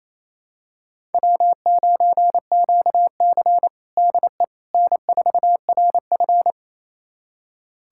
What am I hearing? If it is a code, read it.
Morse code, heard as a keyed tone, about 28 wpm, 700 Hz. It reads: W9QC DE N4RF
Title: USA Callsign DE Callsign - 4 Characters - 28wpm